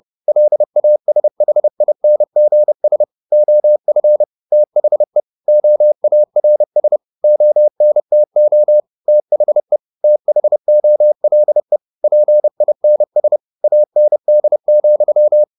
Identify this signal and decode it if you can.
Morse code, heard as a keyed tone, about 30 wpm, 600 Hz. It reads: LASHINGS OF THE OARS ONTO THE THOLE PINS AND,
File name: the old man and the sea / the old man and the sea 1 30wpm